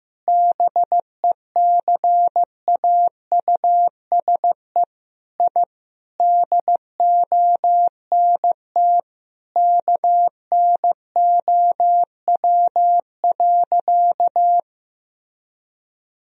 Morse code, heard as a keyed tone, about 15 wpm, 700 Hz. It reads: BECAUSE I DONT KNOW.
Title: Sentences from Top 1000 Words - Review Part 3 - 15wpm